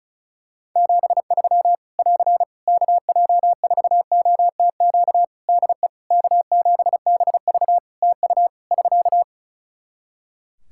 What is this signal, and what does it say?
Morse code, heard as a keyed tone, about 36 wpm, 700 Hz. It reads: 73 <AR> KJ4OTQ DE K7BV TU <SK>
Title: K1USN SST QSO Practice 35wpm